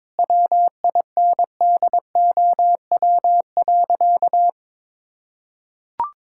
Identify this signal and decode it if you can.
Morse code, heard as a keyed tone, about 22 wpm, 700 Hz. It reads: WINDOW.
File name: practice / 22wpm / Sentences with 2-Word Phrases - 22wpm